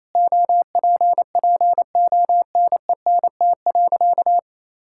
Morse code, heard as a keyed tone, about 28 wpm, 700 Hz. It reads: OPPONENT.
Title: Pangram Challenge v2 - 28wpm